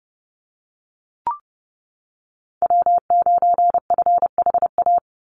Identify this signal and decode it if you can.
Morse code, heard as a keyed tone, about 30 wpm, 700 Hz. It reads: W9FHA
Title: NA SOTA Chaser Callsigns 30wpm